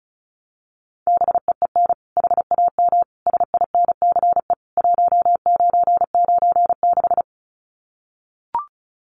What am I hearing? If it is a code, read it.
Morse code, heard as a keyed tone, about 34 wpm, 700 Hz. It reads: BEEN HAM SINCE 1996
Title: Common QSO Phrases 35wpm